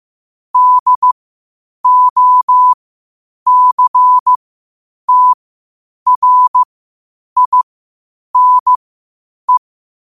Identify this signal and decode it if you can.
Morse code, heard as a keyed tone, about 15 wpm, spaced as for 8 wpm, 1000 Hz. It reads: DOCTRINE
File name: qotdfiles / 8wpm